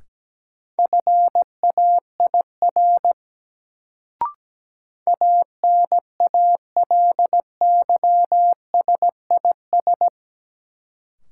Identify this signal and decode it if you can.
Morse code, heard as a keyed tone, about 17 wpm, 700 Hz. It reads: FAIR ANALYSIS
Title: Sentences from Top 1000 Words - Review Part 5 - New Word Drill - 17wpm